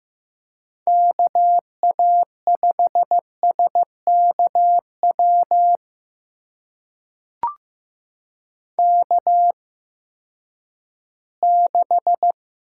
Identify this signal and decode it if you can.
Morse code, heard as a keyed tone, about 15 wpm, spaced as for 7 wpm, 700 Hz. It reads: KA5SK WK6
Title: USA Call Signs - 8x Character Spacing - 15wpm